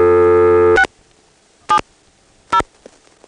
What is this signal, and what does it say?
DTMF, heard as telephone touch tones, tones of about 80 ms. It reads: 670
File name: phonetrips / PrinceWmCoVA5HQ